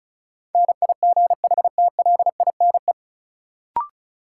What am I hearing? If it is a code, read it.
Morse code, heard as a keyed tone, about 36 wpm, 700 Hz. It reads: NIGHTLINE
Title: Sets of 5 Words - New Word Drill - 35wpm